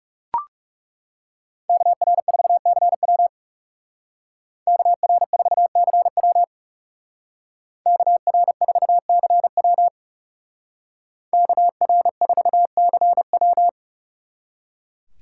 Morse code, heard as a keyed tone, about 36 wpm, 700 Hz. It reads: KR4CW KR4CW KR4CW KR4CW
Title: Speed Racing - USA Call Signs 30wpm